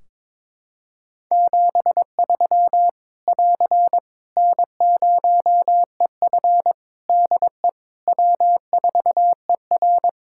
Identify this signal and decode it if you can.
Morse code, heard as a keyed tone, about 22 wpm, 700 Hz. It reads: 73 <AR> N0EF DE W4ER
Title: Practice QSO - Exchange 4 - Ending - 22wpm